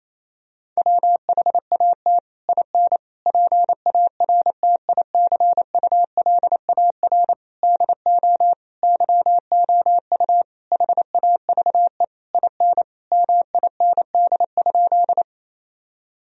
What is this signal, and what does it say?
Morse code, heard as a keyed tone, about 28 wpm, 700 Hz. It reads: WHAT IN PARTICULAR DO YOU HAVE IN MIND?
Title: Sentences from Top 1000 Words - Review Part 4 - 28wpm